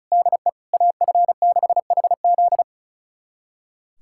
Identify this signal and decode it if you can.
Morse code, heard as a keyed tone, about 34 wpm, 700 Hz. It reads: DE AF6HZ
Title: USA Callsign DE Callsign 35wpm